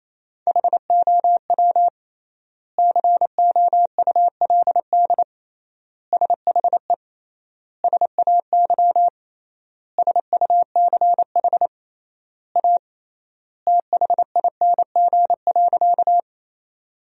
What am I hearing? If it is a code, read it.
Morse code, heard as a keyed tone, about 28 wpm, 700 Hz. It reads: HOW COULD SHE SAY SUCH A THING.
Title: 3x Word-Spacing - Sentences from Top 200 Words - 28wpm